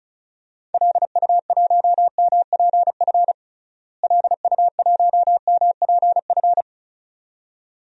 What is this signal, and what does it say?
Morse code, heard as a keyed tone, about 34 wpm, 700 Hz. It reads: LU1MPF LU1MPF
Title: DXCC-100 Call Signs - 6 Characters - Repeated 2x - 35wpm